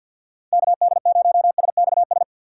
Morse code, heard as a keyed tone, about 49 wpm, 700 Hz. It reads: KD0SXS
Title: USA Call Signs - 6 Characters - 6x Character Spacing - 50wpm